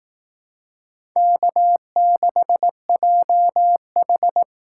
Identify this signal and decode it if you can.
Morse code, heard as a keyed tone, about 18 wpm, 700 Hz. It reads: K6JH
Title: Speed Racing - USA 1x2 Call Signs 15wpm